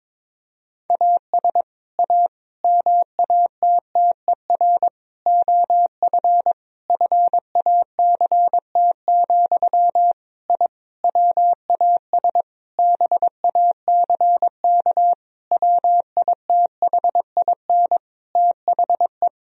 Morse code, heard as a keyed tone, about 22 wpm, 700 Hz. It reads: AS A MATTER OF FACT, I WAS BACK WITHIN THE
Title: Sentences from Top 1000 Words - Review Part 4 - 22wpm